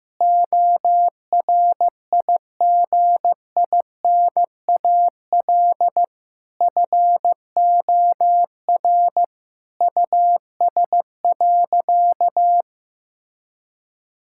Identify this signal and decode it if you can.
Morse code, heard as a keyed tone, about 15 wpm, 700 Hz. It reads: ORIGINAL FOR US.
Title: Sentences from Top 1000 Words - Review Part 6 - 15wpm